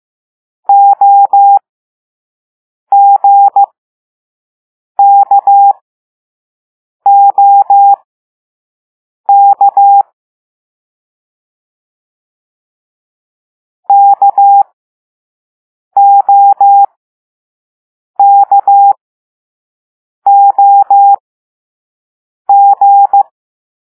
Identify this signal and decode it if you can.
Morse code, heard as a keyed tone, about 15 wpm, spaced as for 5 wpm, 800 Hz. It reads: OGKOK KOKOG
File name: Morse / Level4B